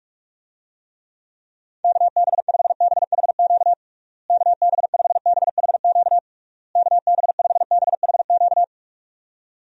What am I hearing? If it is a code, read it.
Morse code, heard as a keyed tone, about 46 wpm, 700 Hz. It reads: KB5BHQ KB5BHQ KB5BHQ
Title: USA Call Signs - 6 Characters - Repeated 3x - 45wpm